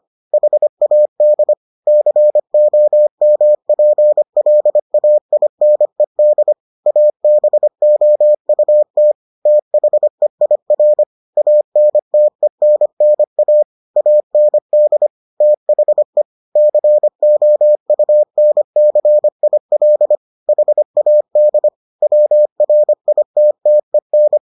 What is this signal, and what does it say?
Morse code, heard as a keyed tone, about 25 wpm, 600 Hz. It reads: HAD COMPLAINED ABOUT THEIR ANTENNA AND THE COUNCIL HAD WRITTEN